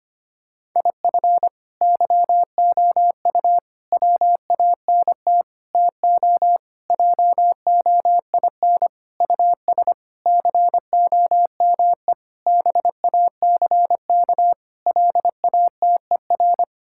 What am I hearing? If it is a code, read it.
Morse code, heard as a keyed tone, about 25 wpm, 700 Hz. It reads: IF YOU WANT TO JOIN US COME BACK LATER
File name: practice / 25wpm / Sentences with 4-Word Phrases - 25wpm